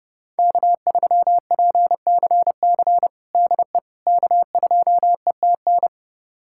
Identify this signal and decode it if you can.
Morse code, heard as a keyed tone, about 30 wpm, 700 Hz. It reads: K3PCC DE K2ETN
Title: USA Callsign DE Callsign - 5 Characters - 30wpm